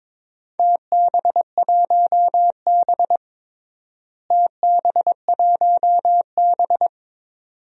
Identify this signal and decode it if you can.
Morse code, heard as a keyed tone, about 22 wpm, 700 Hz. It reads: TB1B TB1B
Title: DXCC-100 Call Signs - 4 Characters - Repeated 2x - 22wpm